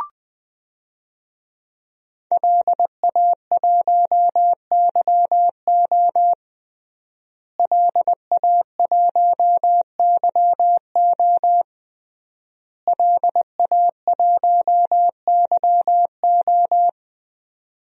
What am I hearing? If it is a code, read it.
Morse code, heard as a keyed tone, about 20 wpm, 700 Hz. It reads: LA1YO LA1YO LA1YO
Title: DXCC-100 Call Signs - 5 Characters - Repeated 3x - 20wpm